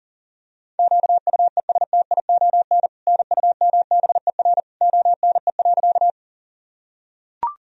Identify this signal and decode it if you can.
Morse code, heard as a keyed tone, about 40 wpm, 700 Hz. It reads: QUESTION NUMBER ONE.
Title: Sentences from Top 1000 Words - Review Part 1 - 40wpm